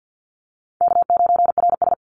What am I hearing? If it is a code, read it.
Morse code, heard as a keyed tone, about 50 wpm, 700 Hz. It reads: K9RS
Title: Easy Call Signs Repeated 2x - 50wpm